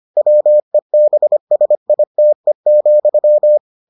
Morse code, heard as a keyed tone, about 25 wpm, 600 Hz. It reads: WEBSITE,